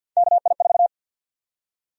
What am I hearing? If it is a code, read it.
Morse code, heard as a keyed tone, about 47 wpm, 700 Hz. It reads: KI4
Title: USA Call Sign Prefixes - 50wpm